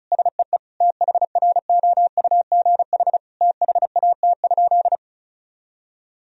Morse code, heard as a keyed tone, about 36 wpm, 700 Hz. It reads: SEE THROUGH THAT?
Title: Sentences from Top 1000 Words - Review Part 1 - 35wpm